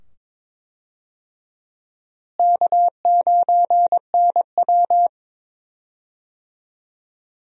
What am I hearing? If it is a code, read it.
Morse code, heard as a keyed tone, about 23 wpm, 700 Hz. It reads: K9NW
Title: USA Call Signs - 4 Characters - Repeated 2x - 22wpm